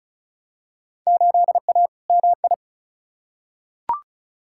Morse code, heard as a keyed tone, about 36 wpm, 700 Hz. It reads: 8A MI
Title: Field Day Exchanges Encore - 35wpm